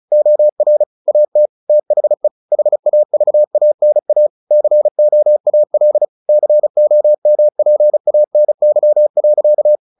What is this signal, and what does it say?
Morse code, heard as a keyed tone, about 34 wpm, 600 Hz. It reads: OR AT THE HAVANA COAL COMPANY.
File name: the old man and the sea 2 35wpm